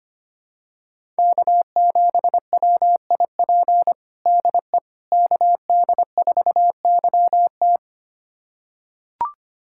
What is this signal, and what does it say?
Morse code, heard as a keyed tone, about 25 wpm, 700 Hz. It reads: K7WIP DE KD4YT
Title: USA Callsign DE Callsign - 5 Characters - 25wpm